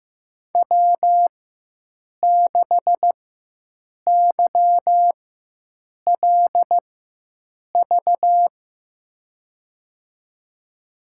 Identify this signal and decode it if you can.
Morse code, heard as a keyed tone, about 15 wpm, spaced as for 7 wpm, 700 Hz. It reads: W6YLV
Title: USA Call Signs - 5 Characters - 4x Character Spacing - 15wpm